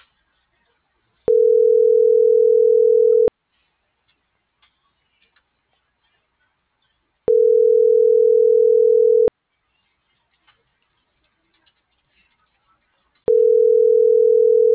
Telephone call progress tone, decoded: ringback tone

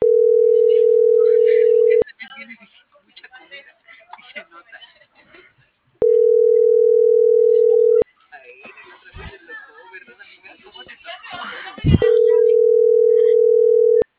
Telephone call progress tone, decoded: ringback tone

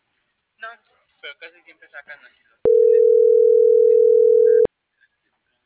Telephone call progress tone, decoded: ringback tone

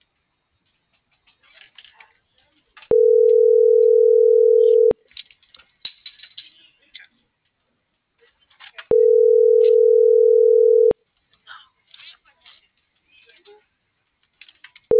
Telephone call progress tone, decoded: ringback tone

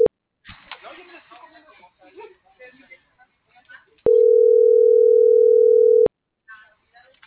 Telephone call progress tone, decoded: ringback tone

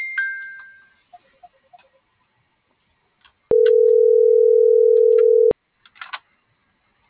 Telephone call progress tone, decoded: ringback tone

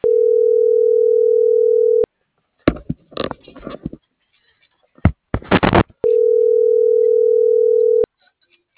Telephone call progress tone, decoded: ringback tone

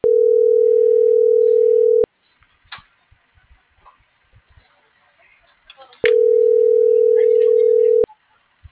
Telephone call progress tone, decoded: ringback tone

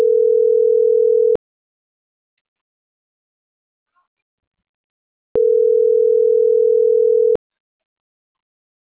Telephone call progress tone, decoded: ringback tone

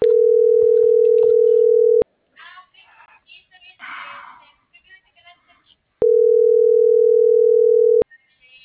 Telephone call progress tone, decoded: ringback tone

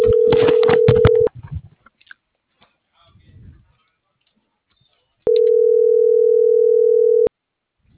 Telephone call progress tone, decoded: ringback tone

